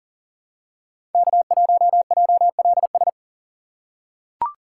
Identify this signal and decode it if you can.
Morse code, heard as a keyed tone, about 37 wpm, 700 Hz. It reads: K1JLS
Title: USA Call Signs - 5 Characters - 4x Character Spacing - 40wpm